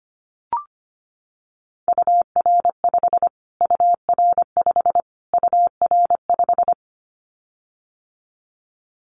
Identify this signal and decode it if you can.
Morse code, heard as a keyed tone, about 25 wpm, 700 Hz. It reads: UR5 UR5 UR5
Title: Callsign Prefixes for Top 25 DXCC - Repeated 3x - 25wpm